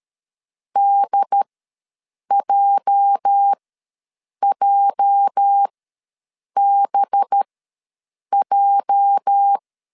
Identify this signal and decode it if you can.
Morse code, heard as a keyed tone, about 13 wpm, spaced as for 7 wpm, 800 Hz. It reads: DJJBJ
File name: LEZIONI / 8 YJ45 / 8 YJ ws 12wpm 3342